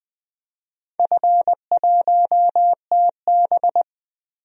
Morse code, heard as a keyed tone, about 20 wpm, 700 Hz. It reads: F1TB
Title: DXCC-100 Call Signs - 4 Characters - 20wpm